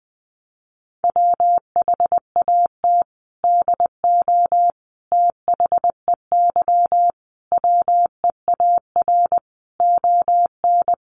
Morse code, heard as a keyed tone, about 20 wpm, 700 Hz. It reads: WHAT DO THEY WEAR ON
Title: Sentences from Top 1500 Words 20wpm